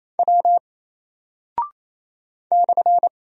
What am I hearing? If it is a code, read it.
Morse code, heard as a keyed tone, about 27 wpm, 700 Hz. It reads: W /
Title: Single Character (Letter:Number:Punctuation:Prosign) 28wpm